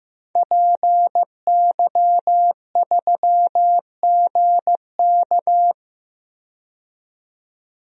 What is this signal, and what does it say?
Morse code, heard as a keyed tone, about 15 wpm, 700 Hz. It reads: PY3GK
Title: DXCC-100 Call Signs - 5 Characters - 15wpm